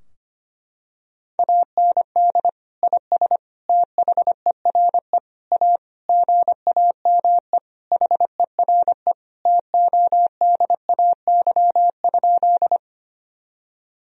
Morse code, heard as a keyed tone, about 25 wpm, 700 Hz. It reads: AND IS THERE A GAME HERE TODAY?
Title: Sentences from Top 1000 Words - Review Part 3 - 25wpm